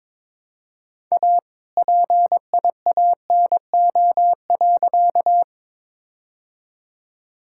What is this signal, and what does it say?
Morse code, heard as a keyed tone, about 22 wpm, 700 Hz. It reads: A PIANO.
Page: morse-code-ninja-practice-22wpm